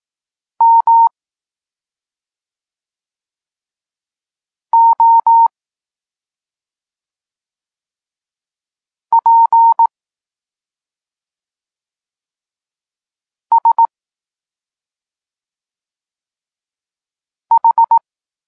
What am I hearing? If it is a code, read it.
Morse code, heard as a keyed tone, about 18 wpm, 920 Hz. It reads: M O P S H
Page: Tekst 32 – Doskonalenie odbioru poznanych liter